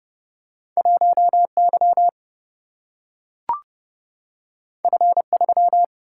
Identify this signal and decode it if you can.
Morse code, heard as a keyed tone, about 29 wpm, 700 Hz. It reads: 1 Y F 3
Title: Single-Letter Single-Number 30wpm